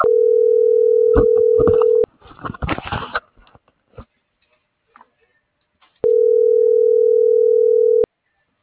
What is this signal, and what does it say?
Ringback tone (North America), 440 + 480 Hz.
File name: CENTRAL SOTOVA / out alejandro.rodriguez